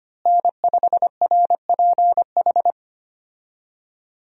Morse code, heard as a keyed tone, about 25 wpm, 700 Hz. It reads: N5RPH
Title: USA Call Signs - 5 Characters - Repeated 2x - 25wpm